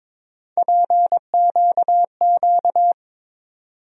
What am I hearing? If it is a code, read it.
Morse code, heard as a keyed tone, about 22 wpm, 700 Hz. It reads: PQQ
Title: USA Call Sign Postfixes - 3 Letters 22wpm